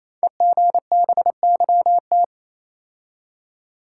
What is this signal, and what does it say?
Morse code, heard as a keyed tone, about 27 wpm, 700 Hz. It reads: EGBYT
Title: Code Groups - 5 Letter:Numbers - 28wpm